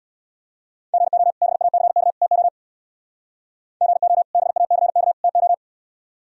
Morse code, heard as a keyed tone, about 64 wpm, 700 Hz. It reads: GO BACK TO GO BACK TO